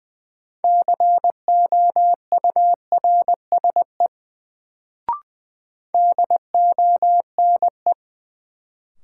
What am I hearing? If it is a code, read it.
Morse code, heard as a keyed tone, about 20 wpm, 700 Hz. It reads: COURSE DONE
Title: Sentences from Top 500 Words - New Word Drill - 20wpm